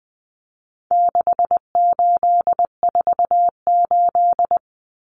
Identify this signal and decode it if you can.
Morse code, heard as a keyed tone, about 20 wpm, 700 Hz. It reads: 6848
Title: Numbers - 4-Digit 20wpm